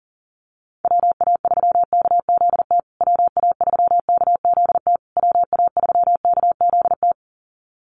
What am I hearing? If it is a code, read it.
Morse code, heard as a keyed tone, about 40 wpm, 700 Hz. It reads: WA3KZT WA3KZT WA3KZT
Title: USA Call Signs Repeated 3x 40wpm